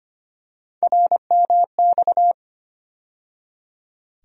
Morse code, heard as a keyed tone, about 25 wpm, 700 Hz. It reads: RMX
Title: USA Call Sign Postfixes - 3 Letters 25wpm